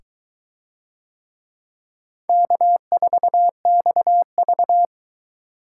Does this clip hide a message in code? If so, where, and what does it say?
Yes: Morse code from 2.3 to 4.8 s, heard as a keyed tone, about 22 wpm, 700 Hz. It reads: K4XV